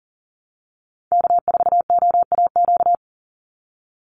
Morse code, heard as a keyed tone, about 40 wpm, 700 Hz. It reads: K4OAQ
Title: Call Signs Repeated 3x - 40wpm